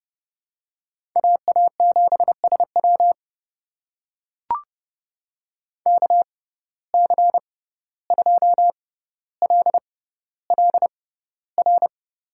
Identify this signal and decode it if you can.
Morse code, heard as a keyed tone, about 30 wpm, spaced as for 15 wpm, 700 Hz. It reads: AA7SW KC2LLR
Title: USA Call Signs - 6x Character Spacing - 30wpm